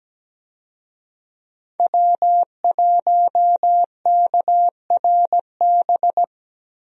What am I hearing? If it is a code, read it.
Morse code, heard as a keyed tone, about 17 wpm, 700 Hz. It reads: W1KRB
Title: USA Call Signs - 5 Characters - Repeated 2x - 17wpm